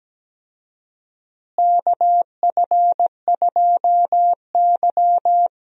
Morse code, heard as a keyed tone, about 17 wpm, 700 Hz. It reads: KF2Y
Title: USA Call Signs - 4 Characters - Repeated 2x - 17wpm